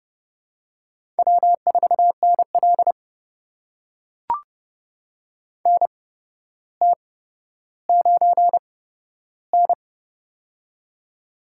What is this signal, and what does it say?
Morse code, heard as a keyed tone, about 30 wpm, spaced as for 13 wpm, 700 Hz. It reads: W4NL NT9N
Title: USA Call Signs - 4 Characters - 8x Character Spacing - 30wpm